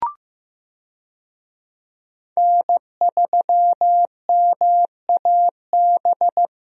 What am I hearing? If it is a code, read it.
Morse code, heard as a keyed tone, about 15 wpm, 700 Hz. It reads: N3MAB
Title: USA Call Signs - 5 Characters - 15wpm